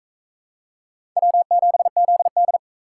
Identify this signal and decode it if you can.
Morse code, heard as a keyed tone, about 39 wpm, 700 Hz. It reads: W7ZD